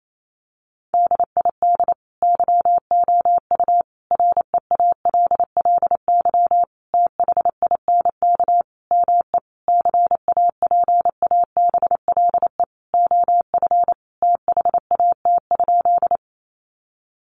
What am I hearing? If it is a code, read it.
Morse code, heard as a keyed tone, about 28 wpm, 700 Hz. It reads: DID YOU REALLY THINK ME CAPABLE OF THAT?
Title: Sentences from Top 1200 Words 28wpm